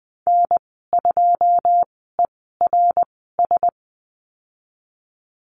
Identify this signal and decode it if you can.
Morse code, heard as a keyed tone, about 20 wpm, spaced as for 15 wpm, 700 Hz. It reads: N2ERS
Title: 088 Lesson 15g - Number 5 - Call Signs 20wpm